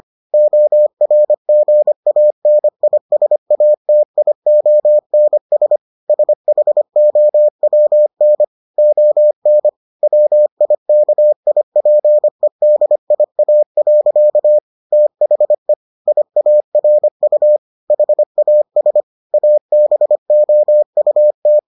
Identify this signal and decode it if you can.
Morse code, heard as a keyed tone, about 25 wpm, 600 Hz. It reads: ORGANISATIONS SHOWN ON WIKIPEDIA. THE IARU HAS ABOUT